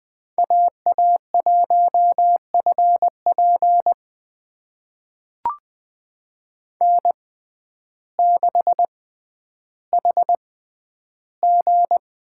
Morse code, heard as a keyed tone, about 20 wpm, spaced as for 11 wpm, 700 Hz. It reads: AA1FP N6HG
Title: USA Call Signs - 5 Characters - 6x Character Spacing - 20wpm